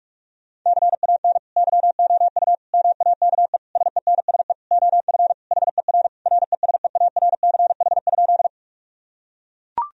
Morse code, heard as a keyed tone, about 41 wpm, 700 Hz. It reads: CAN YOU MAKE SENSE OF HER RESEARCH?
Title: Sentences from Top 1000 Words - Review Part 4 - 45wpm